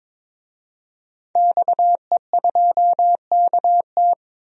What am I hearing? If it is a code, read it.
Morse code, heard as a keyed tone, about 21 wpm, 700 Hz. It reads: XE2KT
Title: DXCC-100 Call Signs - 5 Characters - 22wpm